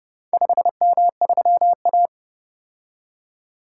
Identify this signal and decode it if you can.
Morse code, heard as a keyed tone, about 28 wpm, 700 Hz. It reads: 5M3A